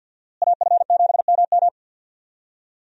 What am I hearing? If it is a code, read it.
Morse code, heard as a keyed tone, about 50 wpm, 700 Hz. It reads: AF7MM
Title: USA Call Signs - 5 Characters - Repeated 2x - 50wpm